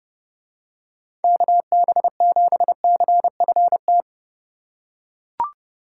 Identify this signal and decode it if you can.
Morse code, heard as a keyed tone, about 30 wpm, 700 Hz. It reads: KB7CFT